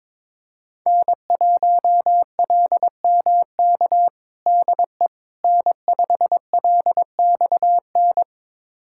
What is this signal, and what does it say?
Morse code, heard as a keyed tone, about 22 wpm, 700 Hz. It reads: N1LMK DE N5LXN
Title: USA Callsign DE Callsign - 5 Characters - 22wpm